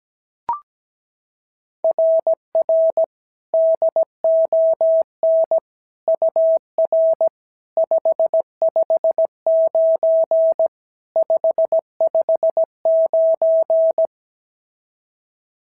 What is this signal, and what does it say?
Morse code, heard as a keyed tone, about 16 wpm, 650 Hz. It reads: RR DON UR 559 559
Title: SOTA QSO Practice 17wpm